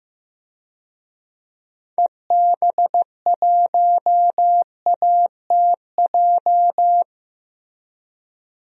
Morse code, heard as a keyed tone, about 15 wpm, 700 Hz. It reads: EB1ATJ
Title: DXCC-100 Call Signs - 6 Characters - Repeated 2x - 15wpm